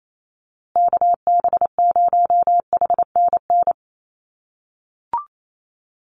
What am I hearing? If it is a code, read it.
Morse code, heard as a keyed tone, about 28 wpm, 700 Hz. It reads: KB0HNN
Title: USA Call Signs Repeated 2x 28wpm